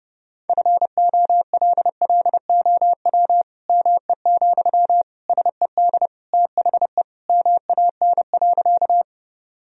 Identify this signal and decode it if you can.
Morse code, heard as a keyed tone, about 32 wpm, 700 Hz. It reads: FOLLOW ME, SED THE MAN.
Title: Sentences from Top 200 Words and Abbreviations 30wpm